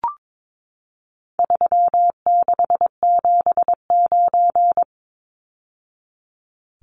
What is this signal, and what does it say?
Morse code, heard as a keyed tone, about 22 wpm, 700 Hz. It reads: 3679
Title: Numbers - 4-Digit 22wpm